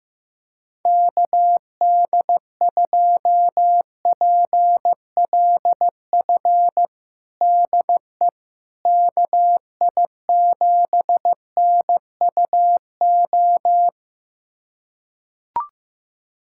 Morse code, heard as a keyed tone, about 15 wpm, 700 Hz. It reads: KD2PLF DE KI7NUO
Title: USA Callsign DE Callsign - 6 Characters - 15wpm